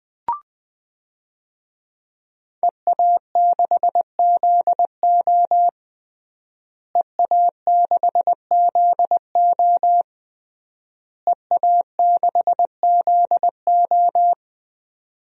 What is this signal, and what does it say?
Morse code, heard as a keyed tone, about 20 wpm, 700 Hz. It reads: EA6ZO EA6ZO EA6ZO
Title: DXCC-100 Call Signs - 5 Characters - Repeated 3x - 20wpm